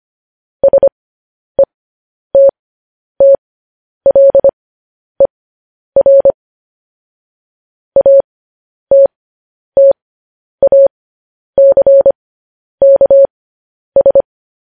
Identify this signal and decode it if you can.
Morse code, heard as a keyed tone, about 25 wpm, spaced as for 10 wpm, 550 Hz. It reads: SETTLER ATTACKS